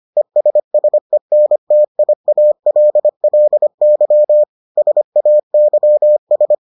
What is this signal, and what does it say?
Morse code, heard as a keyed tone, about 25 wpm, 600 Hz. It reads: ESSENTIALLY SAYS